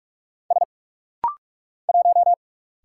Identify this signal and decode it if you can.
Morse code, heard as a keyed tone, about 46 wpm, 700 Hz. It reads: S 1
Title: Single Letter-Number Alt - Rapid-Fire 45wpm